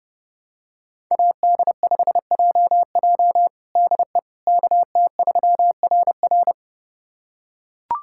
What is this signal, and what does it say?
Morse code, heard as a keyed tone, about 30 wpm, 700 Hz. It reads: AD5JJ DE KT3RR
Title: USA Callsign DE Callsign - 5 Characters - 30wpm